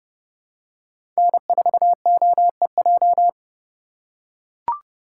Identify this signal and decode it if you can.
Morse code, heard as a keyed tone, about 30 wpm, 700 Hz. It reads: N4OEJ
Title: USA Call Signs - 5 Characters - 4x Character Spacing - 30wpm